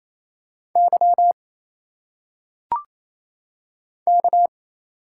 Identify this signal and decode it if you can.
Morse code, heard as a keyed tone, about 28 wpm, 700 Hz. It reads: Y K
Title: Single Letters - Phonetic Pronunciation 28wpm